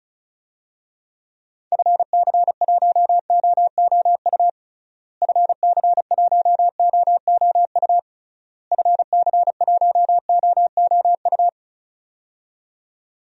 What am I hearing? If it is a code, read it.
Morse code, heard as a keyed tone, about 34 wpm, 700 Hz. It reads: FC1OOU FC1OOU FC1OOU
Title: DXCC-100 Call Signs - 6 Characters - Repeated 3x - 35wpm